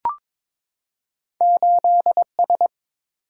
Morse code, heard as a keyed tone, about 22 wpm, 700 Hz. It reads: 8 S
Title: Single-Letter Single-Number 22wpm